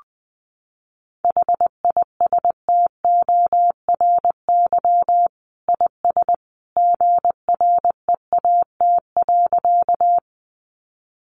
Morse code, heard as a keyed tone, about 20 wpm, 700 Hz. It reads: HISTORY IS GREAT.
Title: Sentences from Top 600 Words 20wpm